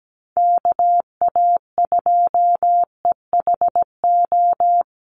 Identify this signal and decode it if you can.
Morse code, heard as a keyed tone, about 17 wpm, 700 Hz. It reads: KA2EHO
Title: USA Call Signs 17wpm